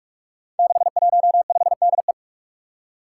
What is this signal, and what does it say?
Morse code, heard as a keyed tone, about 45 wpm, 700 Hz. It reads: 615DE
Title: Five Challenging Letter-Number Combinations 45wpm